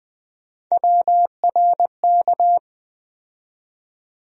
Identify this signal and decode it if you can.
Morse code, heard as a keyed tone, about 20 wpm, 700 Hz. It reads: WRK
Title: USA Call Sign Postfixes - 3 Letters 20wpm